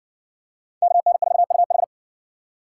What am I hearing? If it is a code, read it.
Morse code, heard as a keyed tone, about 63 wpm, 700 Hz. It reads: KN4UH